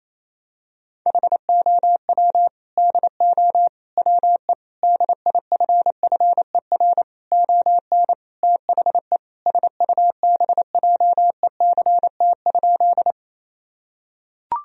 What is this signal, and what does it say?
Morse code, heard as a keyed tone, about 28 wpm, 700 Hz. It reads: HOW DO WE DIFFER ON THE SUBJECT?
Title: Sentences from Top 1000 Words - Review Part 1 - 28wpm